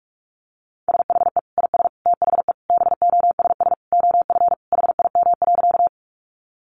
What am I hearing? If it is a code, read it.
Morse code, heard as a keyed tone, about 44 wpm, 700 Hz. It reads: SHE IS THE BOSS OF HIM.
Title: Sentences from Top 700 Words 45wpm